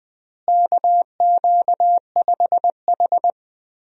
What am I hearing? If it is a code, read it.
Morse code, heard as a keyed tone, about 21 wpm, 700 Hz. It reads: KQ5H
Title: USA Call Signs - 4 Characters - Repeated 3x - 20wpm